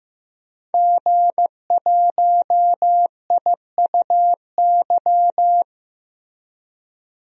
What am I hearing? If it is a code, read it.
Morse code, heard as a keyed tone, about 15 wpm, 700 Hz. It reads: G1IUY
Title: DXCC-100 Call Signs - 5 Characters - 15wpm